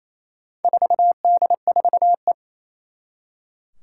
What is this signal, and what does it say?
Morse code, heard as a keyed tone, about 29 wpm, 700 Hz. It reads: 4D4E
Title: Four Challenging Letter-Number Combinations 28wpm